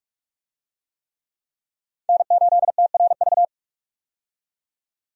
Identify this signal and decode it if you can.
Morse code, heard as a keyed tone, about 46 wpm, 700 Hz. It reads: N8TRV
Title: USA Call Signs - 5 Characters - 45wpm